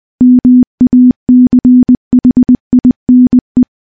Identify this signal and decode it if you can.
Morse code, heard as a keyed tone, about 20 wpm, 250 Hz. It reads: MACHINE
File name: morse-training-20WPM